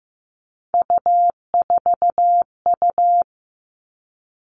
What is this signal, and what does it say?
Morse code, heard as a keyed tone, about 15 wpm, 700 Hz. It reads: U4U
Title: Three Challenging Letter-Number Combinations 15wpm